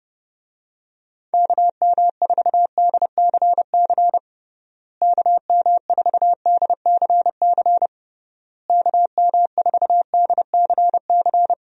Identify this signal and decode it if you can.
Morse code, heard as a keyed tone, about 30 wpm, 700 Hz. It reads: KM4DCC KM4DCC KM4DCC
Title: USA Call Signs - 6 Characters - Repeated 3x - 30wpm